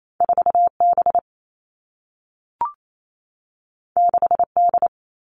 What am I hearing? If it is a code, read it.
Morse code, heard as a keyed tone, about 28 wpm, 700 Hz. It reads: 4B 6D
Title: All Cut Numbers - 2-Digit 28wpm